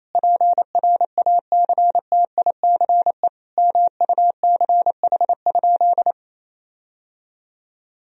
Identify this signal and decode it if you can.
Morse code, heard as a keyed tone, about 27 wpm, 700 Hz. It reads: PRACTICE MUCH?
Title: Sentences from Top 1000 Words - Review Part 4 - 28wpm